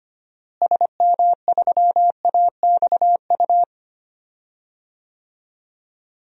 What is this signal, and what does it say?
Morse code, heard as a keyed tone, about 25 wpm, 700 Hz. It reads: SM3AXU
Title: DXCC-100 Call Signs - 6 Characters - Repeated 3x - 25wpm